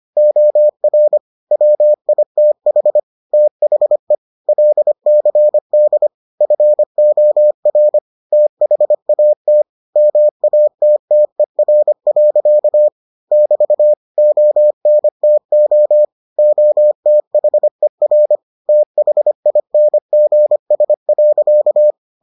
Morse code, heard as a keyed tone, about 25 wpm, 600 Hz. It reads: OR WITH THE LCD FOR THAT MATTER. <BT> ONTO OTHER THINGS.